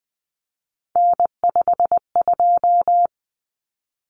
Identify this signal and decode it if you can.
Morse code, heard as a keyed tone, about 20 wpm, 700 Hz. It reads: N52
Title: Common Cut Numbers - 3-Digit 20wpm